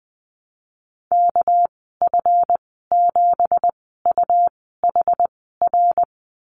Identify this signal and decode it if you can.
Morse code, heard as a keyed tone, about 20 wpm, spaced as for 14 wpm, 700 Hz. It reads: KF7UHR
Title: 245 Lesson 39g - Letter z - Call Signs 20wpm